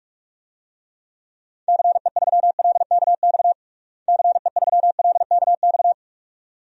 Morse code, heard as a keyed tone, about 46 wpm, 700 Hz. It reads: CE3LKX CE3LKX
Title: DXCC-100 Call Signs - 6 Characters - Repeated 2x - 45wpm